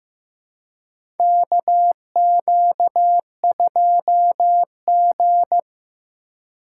Morse code, heard as a keyed tone, about 15 wpm, 700 Hz. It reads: KQ2G
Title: USA Call Signs - 4 Characters - 3x Character Spacing - 15wpm